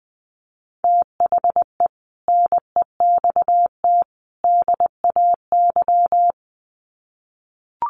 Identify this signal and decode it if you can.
Morse code, heard as a keyed tone, about 20 wpm, 700 Hz. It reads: THE NEXT DAY
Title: Sets of 3 Words 20wpm